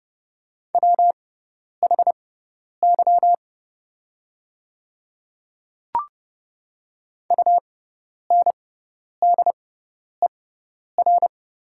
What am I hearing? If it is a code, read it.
Morse code, heard as a keyed tone, about 30 wpm, spaced as for 10 wpm, 700 Hz. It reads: WHY UNDER
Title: 6x Character Spacing - Top 200 Words - 30wpm